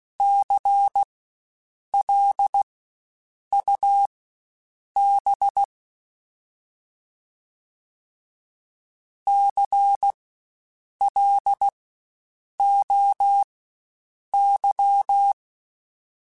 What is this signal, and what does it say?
Morse code, heard as a keyed tone, about 16 wpm, spaced as for 7 wpm, 800 Hz. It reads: CLUB CLOY
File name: Spaced Alphas / Spaced Alphas mp3s / Cs sp @15